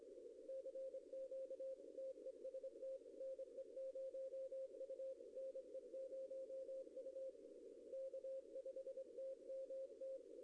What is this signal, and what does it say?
Morse code, heard as a keyed tone, about 25 wpm, 530 Hz. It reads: CQ TEST NE0U NE0U K5TMT
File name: contests / arrl-160-k5tr-op-w5tn -0100